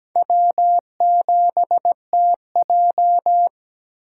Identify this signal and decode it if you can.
Morse code, heard as a keyed tone, about 17 wpm, 700 Hz. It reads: W7TJ
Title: USA Call Signs - 4 Characters - 6x Character Spacing - 17wpm